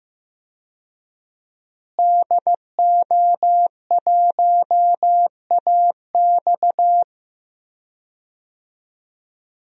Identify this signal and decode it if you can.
Morse code, heard as a keyed tone, about 15 wpm, 700 Hz. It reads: DO1AX
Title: DXCC-100 Call Signs - 5 Characters - Repeated 3x - 15wpm